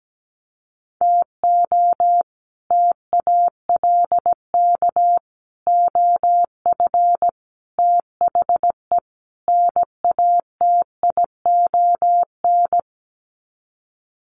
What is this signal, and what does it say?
Morse code, heard as a keyed tone, about 17 wpm, 700 Hz. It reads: TO TALK OF THE NATION
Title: Sets of 5 Words 17wpm